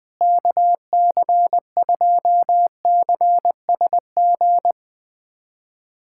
Morse code, heard as a keyed tone, about 20 wpm, 700 Hz. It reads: KC2CSG
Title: USA Callsign DE Callsign 20wpm